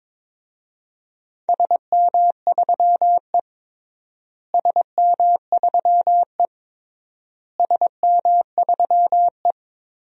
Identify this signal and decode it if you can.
Morse code, heard as a keyed tone, about 22 wpm, 700 Hz. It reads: SM3E SM3E SM3E
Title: DXCC-100 Call Signs - 4 Characters - Repeated 3x - 22wpm